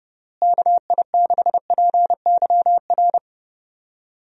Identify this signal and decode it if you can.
Morse code, heard as a keyed tone, about 32 wpm, 700 Hz. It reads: KI6PYR
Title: USA Call Signs - 6 Characters - Repeated 2x - 30wpm